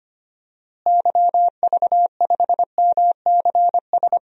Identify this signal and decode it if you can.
Morse code, heard as a keyed tone, about 25 wpm, 700 Hz. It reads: YV5MCS
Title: DXCC-100 Call Signs - 6 Characters - Repeated 2x - 25wpm